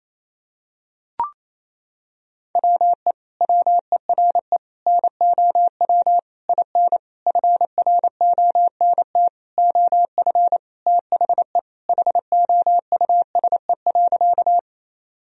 Morse code, heard as a keyed tone, about 28 wpm, 700 Hz. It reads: WE WERE NOW IN FRONT OF THE HOUSE.
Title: Sentences from Top 1000 Words - Review Part 2 - 28wpm